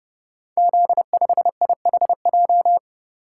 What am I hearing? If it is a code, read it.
Morse code, heard as a keyed tone, about 28 wpm, 700 Hz. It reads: Z5IHJ
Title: Code Groups - 5 Letter:Numbers - 30wpm